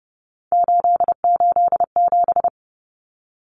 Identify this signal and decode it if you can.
Morse code, heard as a keyed tone, about 30 wpm, 700 Hz. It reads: 887
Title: Numbers - 3-Digit 30wpm